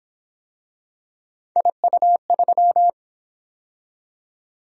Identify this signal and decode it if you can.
Morse code, heard as a keyed tone, about 26 wpm, 700 Hz. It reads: IU3
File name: Speed-Racing - Three Letter-Number Combinations - 22wpm